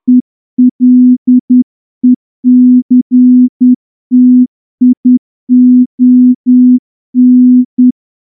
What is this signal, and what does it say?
Morse code, heard as a keyed tone, about 10 wpm, 250 Hz. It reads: ELECTION
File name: morse-training-10WPM